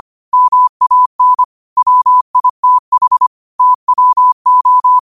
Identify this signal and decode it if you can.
Morse code, heard as a keyed tone, about 25 wpm, 1000 Hz. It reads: MAN WITH TWO
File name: qotdfiles / 25wpm